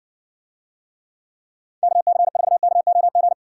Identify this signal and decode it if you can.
Morse code, heard as a keyed tone, about 60 wpm, 700 Hz. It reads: KC4KOG